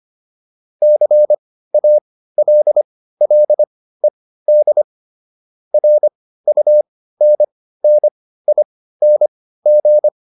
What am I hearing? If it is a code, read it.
Morse code, heard as a keyed tone, about 25 wpm, spaced as for 15 wpm, 600 Hz. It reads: CALLED RUNNING